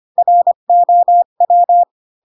Morse code, heard as a keyed tone, about 25 wpm, 700 Hz. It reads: ROW
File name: Pinocchio / Pinocchio ch7 20 25 ik1hge